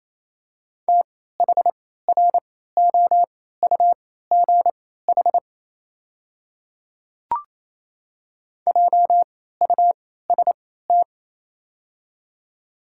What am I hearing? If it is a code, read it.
Morse code, heard as a keyed tone, about 28 wpm, spaced as for 16 wpm, 700 Hz. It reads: THROUGH JUST
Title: 3x Character Spacing - Top 300 Words - 28wpm